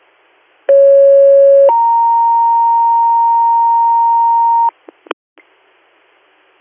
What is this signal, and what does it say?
Two-tone page, heard as a one-second tone, then a three-second tone, about 550 then 930 Hz. It is heard from 0.7 to 4.7 s.